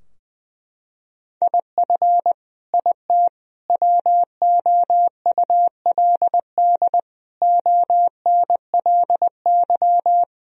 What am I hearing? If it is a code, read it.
Morse code, heard as a keyed tone, about 21 wpm, 700 Hz. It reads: IF IT WOULD ONLY
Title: Sentences from Top 1000 Words - Review Part 3 - 20wpm